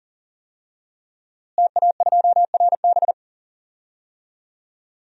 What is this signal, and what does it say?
Morse code, heard as a keyed tone, about 40 wpm, 700 Hz. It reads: TA2RB
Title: DXCC-100 Call Signs - 5 Characters - Repeated 2x - 40wpm